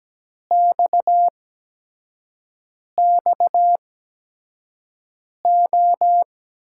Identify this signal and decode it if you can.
Morse code, heard as a keyed tone, about 17 wpm, spaced as for 5 wpm, 700 Hz. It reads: XXO
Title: USA Call Signs - 5 Characters - 8x Character Spacing - 17wpm